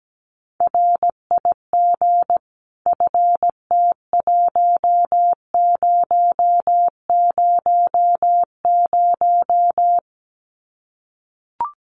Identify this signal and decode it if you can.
Morse code, heard as a keyed tone, about 17 wpm, 700 Hz. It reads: RIG FT1000
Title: Common QSO Phrases 17wpm